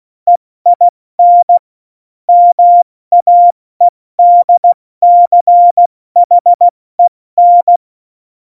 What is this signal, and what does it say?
Morse code, heard as a keyed tone, about 16 wpm, 700 Hz. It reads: EIN MAEDCHEN
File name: Morsekurs / Witze / Witz 31. Oktober 17, Tempo 70